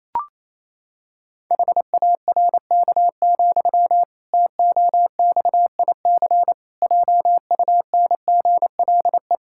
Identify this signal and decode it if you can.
Morse code, heard as a keyed tone, about 28 wpm, 700 Hz. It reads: HARK, TOXIC JUNGLE
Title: Pangram Challenge v2 - 28wpm